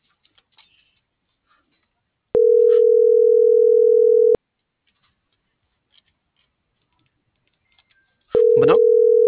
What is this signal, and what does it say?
Ringback tone (North America), 440 + 480 Hz.